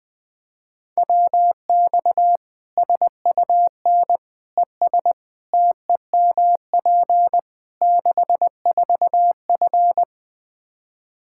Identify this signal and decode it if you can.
Morse code, heard as a keyed tone, about 20 wpm, 700 Hz. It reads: WX SUN ES TEMP 64F
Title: Practice QSO - Exchange 2 - Info - 20wpm